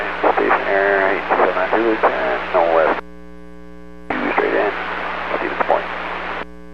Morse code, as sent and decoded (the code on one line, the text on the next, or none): none